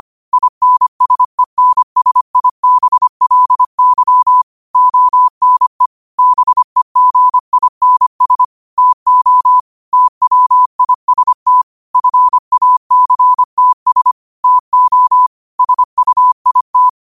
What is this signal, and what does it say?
Morse code, heard as a keyed tone, about 25 wpm, 1000 Hz. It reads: INSENSIBLY ONE BEGINS TO TWIST FACTS TO SUIT